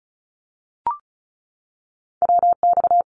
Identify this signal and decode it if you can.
Morse code, heard as a keyed tone, about 36 wpm, 700 Hz. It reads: WX
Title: Abbreviations 35wpm